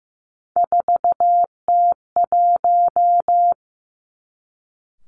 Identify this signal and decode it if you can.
Morse code, heard as a keyed tone, about 15 wpm, 700 Hz. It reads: 4T1
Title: Three Challenging Letter-Number Combinations 15wpm